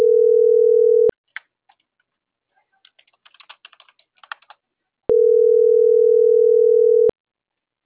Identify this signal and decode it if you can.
Ringback tone (North America), 440 + 480 Hz.
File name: CENTRAL MABE